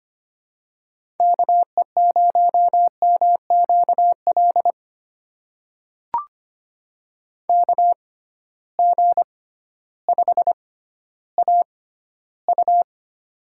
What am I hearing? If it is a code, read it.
Morse code, heard as a keyed tone, about 25 wpm, spaced as for 14 wpm, 700 Hz. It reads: KE0MQL KG5AU
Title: USA Call Signs - 6x Character Spacing - 25wpm